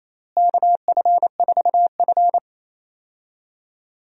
Morse code, heard as a keyed tone, about 28 wpm, 700 Hz. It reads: KF4F
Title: USA Call Signs - 4 Characters - Repeated 3x - 28wpm